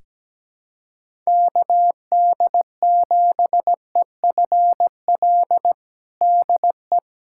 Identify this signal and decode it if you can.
Morse code, heard as a keyed tone, about 17 wpm, 700 Hz. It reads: KD7EFL DE
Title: USA Callsign DE Callsign - 6 Characters - 17wpm